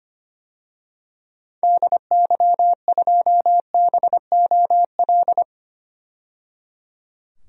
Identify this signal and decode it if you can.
Morse code, heard as a keyed tone, about 25 wpm, 700 Hz. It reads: DY2BOL
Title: DXCC-100 Call Signs - 6 Characters - 25wpm